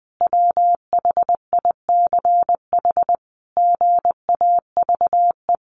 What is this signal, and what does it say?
Morse code, heard as a keyed tone, about 20 wpm, 700 Hz. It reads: WHICH GAVE
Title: Sentences from Top 1500 Words 20wpm